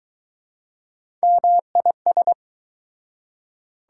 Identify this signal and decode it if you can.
Morse code, heard as a keyed tone, about 24 wpm, 700 Hz. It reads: MIS